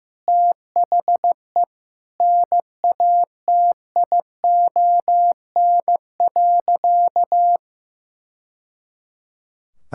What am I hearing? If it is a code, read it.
Morse code, heard as a keyed tone, about 15 wpm, 700 Hz. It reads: THE NATION.
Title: Sentences from Top 1000 Words - Review Part 6 - 15wpm